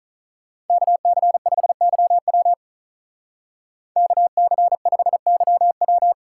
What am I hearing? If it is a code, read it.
Morse code, heard as a keyed tone, about 43 wpm, 700 Hz. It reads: KC5YW KC5YW
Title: Speed Racing - USA Call Signs 35wpm